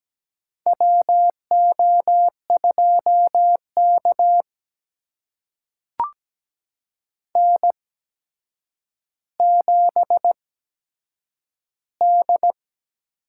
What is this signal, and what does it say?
Morse code, heard as a keyed tone, about 17 wpm, spaced as for 8 wpm, 700 Hz. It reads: WO2K N7D
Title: USA Call Signs - 8x Character Spacing - 17wpm